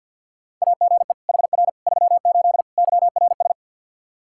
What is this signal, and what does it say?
Morse code, heard as a keyed tone, about 49 wpm, 700 Hz. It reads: AGE HR 38 YRS
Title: Practice QSO - Exchange 3 - More Info - 50wpm